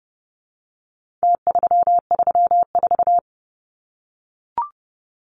Morse code, heard as a keyed tone, about 30 wpm, 700 Hz. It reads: T334
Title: Numbers - Contest Style - 4-Digit 30wpm